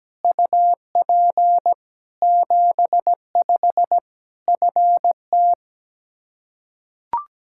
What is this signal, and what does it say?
Morse code, heard as a keyed tone, about 17 wpm, 700 Hz. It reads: UP 75 FT
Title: K1USN SST QSO Practice 17wpm